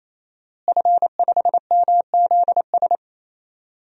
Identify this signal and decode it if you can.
Morse code, heard as a keyed tone, about 28 wpm, 700 Hz. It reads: F5MZS
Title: DXCC-100 Call Signs - 5 Characters - Repeated 3x - 28wpm